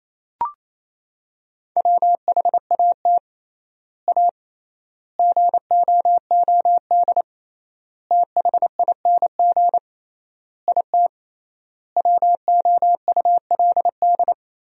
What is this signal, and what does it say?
Morse code, heard as a keyed tone, about 28 wpm, 700 Hz. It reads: WHAT A GOOD THING IT WOULD
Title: 3x Word-Spacing - Sentences from Top 200 Words - 28wpm